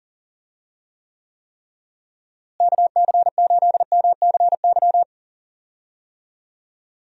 Morse code, heard as a keyed tone, about 38 wpm, 700 Hz. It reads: KC8MCY
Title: USA Call Signs - 6 Characters - Repeated 3x - 40wpm